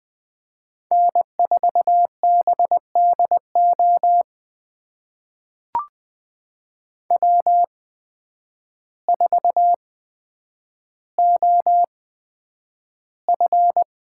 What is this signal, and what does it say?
Morse code, heard as a keyed tone, about 20 wpm, spaced as for 9 wpm, 700 Hz. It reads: N4BDO W4OF
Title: USA Call Signs - 5 Characters - 8x Character Spacing - 20wpm